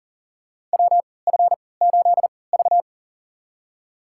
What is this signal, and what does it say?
Morse code, heard as a keyed tone, about 40 wpm, spaced as for 23 wpm, 700 Hz. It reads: WF8V
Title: USA Call Signs - 4 Characters - 3x Character Spacing - 40wpm